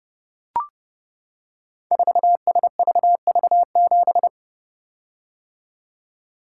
Morse code, heard as a keyed tone, about 30 wpm, 700 Hz. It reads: 4SVV7
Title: Code Groups - 5 Letter:Numbers - 30wpm